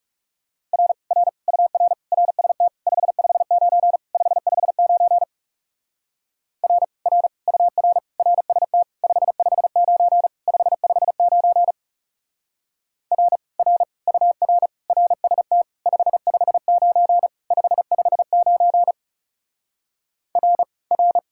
Morse code, heard as a keyed tone, about 37 wpm, 700 Hz. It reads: R R UR RST 559 559 R R UR RST 559 559 R R UR RST 559 559 R R
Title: Speed-Racing Common QSO Phrases - 30wpm